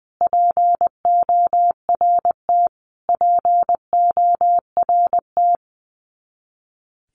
Morse code, heard as a keyed tone, about 20 wpm, 700 Hz. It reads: PORT PORT
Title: Top 200 Words Repeated 20wpm